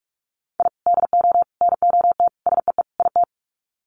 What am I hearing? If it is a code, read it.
Morse code, heard as a keyed tone, about 44 wpm, 700 Hz. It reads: I DO NOT SEE IT
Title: Sets of 5 Words Encore2 45wpm